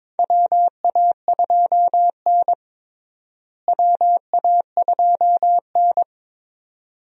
Morse code, heard as a keyed tone, about 22 wpm, 700 Hz. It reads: WA2N WA2N
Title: USA Call Signs - 4 Characters - Repeated 2x - 22wpm